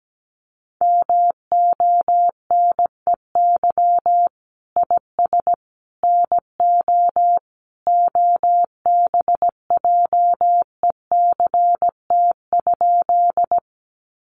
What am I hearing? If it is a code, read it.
Morse code, heard as a keyed tone, about 17 wpm, 700 Hz. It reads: MONEY IS NO OBJECT?
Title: Sentences from Top 500 Words 17wpm